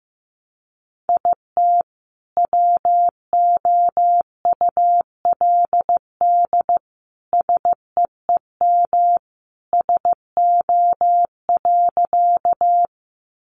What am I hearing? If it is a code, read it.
Morse code, heard as a keyed tone, about 15 wpm, 700 Hz. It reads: IT WOULD SEEM SO.
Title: Sentences from Top 300 Words 15wpm